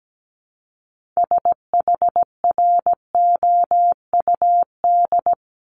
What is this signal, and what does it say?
Morse code, heard as a keyed tone, about 17 wpm, 700 Hz. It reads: SHROUD
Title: Multi-letter Phonemes 17wpm